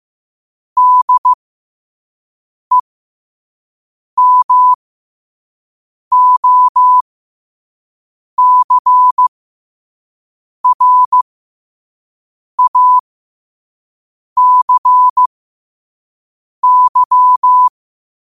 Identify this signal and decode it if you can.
Morse code, heard as a keyed tone, about 15 wpm, spaced as for 5 wpm, 1000 Hz. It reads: DEMOCRACY
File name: qotdfiles / 5wpm